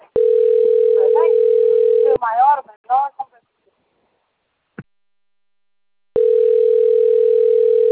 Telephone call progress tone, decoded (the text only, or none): ringback tone